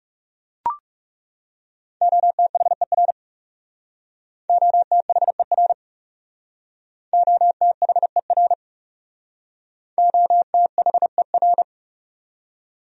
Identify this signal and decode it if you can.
Morse code, heard as a keyed tone, about 37 wpm, 700 Hz. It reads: OTHER OTHER OTHER OTHER
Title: Speed-Racing Top 300 Words 30wpm